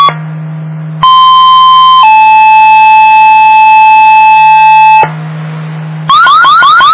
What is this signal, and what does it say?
Two-tone page, heard as a one-second tone, then a three-second tone, about 1040 then 870 Hz. It is heard from 1.0 to 5.0 s.
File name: Inlet